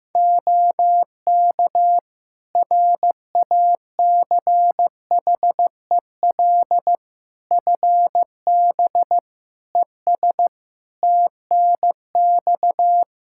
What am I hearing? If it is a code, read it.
Morse code, heard as a keyed tone, about 15 wpm, 700 Hz. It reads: OK RACHEL FB ES TNX
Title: Practice QSO - Exchange 2 - Info - 15wpm